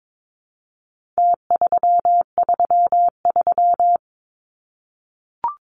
Morse code, heard as a keyed tone, about 22 wpm, 700 Hz. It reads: T333
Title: Numbers - Contest Style - 4-Digit 22wpm